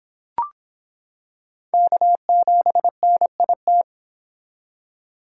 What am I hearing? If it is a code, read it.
Morse code, heard as a keyed tone, about 26 wpm, 700 Hz. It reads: K7NIT